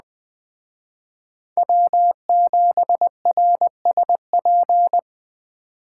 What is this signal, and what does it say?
Morse code, heard as a keyed tone, about 20 wpm, 700 Hz. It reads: W7RSP
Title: Speed Racing - USA 1x3 Call Signs 17wpm